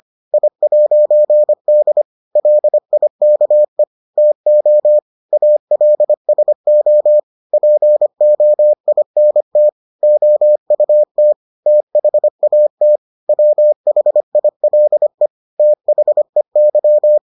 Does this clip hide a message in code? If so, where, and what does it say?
Yes: Morse code from 0.3 to 17.2 s, heard as a keyed tone, about 25 wpm, 600 Hz. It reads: I'D LIKE TO ALSO POINT OUT THAT WHILE THEY